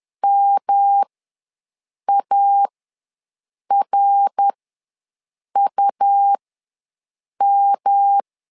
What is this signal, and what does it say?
Morse code, heard as a keyed tone, about 10 wpm, spaced as for 6 wpm, 800 Hz. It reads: MARUM
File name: LEZIONI / 3 MWRO / 3 MWR ws 10wpm 2151